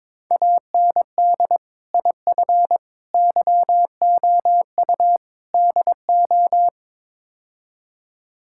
Morse code, heard as a keyed tone, about 22 wpm, 700 Hz. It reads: AND IF YOU DO
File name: practice / 22wpm / Build A Phrase - 5-Word Phrases - 22wpm